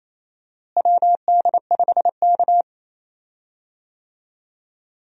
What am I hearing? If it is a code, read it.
Morse code, heard as a keyed tone, about 27 wpm, 700 Hz. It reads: WD5K
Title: USA Call Signs - 4 Characters - Repeated 3x - 28wpm